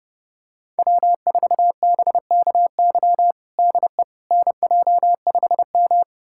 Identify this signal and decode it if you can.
Morse code, heard as a keyed tone, about 32 wpm, 700 Hz. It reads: W4BKY DE NJ5M
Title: USA Callsign DE Callsign 30wpm